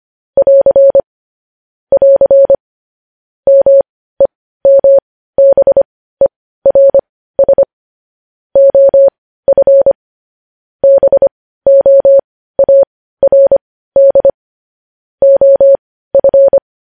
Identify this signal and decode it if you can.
Morse code, heard as a keyed tone, about 25 wpm, spaced as for 15 wpm, 550 Hz. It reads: <AR> <AR> MEMBERS OF BOARD OF
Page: news-headlines-in-morse-code-at-15-wpm